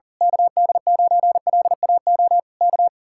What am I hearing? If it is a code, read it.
Morse code, heard as a keyed tone, about 40 wpm, 700 Hz. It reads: KD9LAO K
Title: Speed-Racing Common QSO Phrases - 40wpm